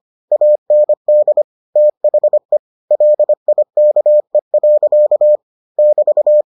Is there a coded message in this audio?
Yes: Morse code from 0.3 to 6.4 s, heard as a keyed tone, about 25 wpm, 600 Hz. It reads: AND THE LIKE. <BT>